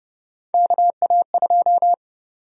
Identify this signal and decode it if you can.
Morse code, heard as a keyed tone, about 30 wpm, 700 Hz. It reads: KA2
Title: USA Call Sign Prefixes - 30wpm